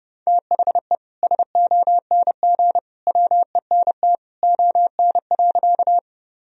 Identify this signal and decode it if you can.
Morse code, heard as a keyed tone, about 30 wpm, 700 Hz. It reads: THE SONG WENT ON.
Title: Sentences from Top 1000 Words - Review Part 2 - 30wpm